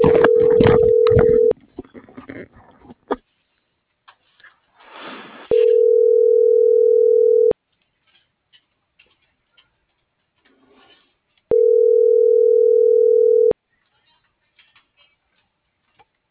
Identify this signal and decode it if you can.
Ringback tone (North America), 440 + 480 Hz.